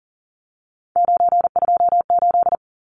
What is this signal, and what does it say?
Morse code, heard as a keyed tone, about 40 wpm, 700 Hz. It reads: 928
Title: Numbers - 3-Digit 40wpm